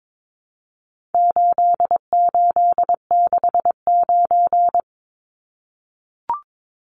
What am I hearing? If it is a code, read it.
Morse code, heard as a keyed tone, about 22 wpm, 700 Hz. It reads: 8869
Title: Numbers - 4-Digit 22wpm